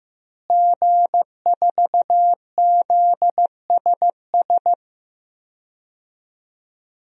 Morse code, heard as a keyed tone, about 15 wpm, 700 Hz. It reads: G4ZSS
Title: DXCC-100 Call Signs - 5 Characters - Repeated 2x - 15wpm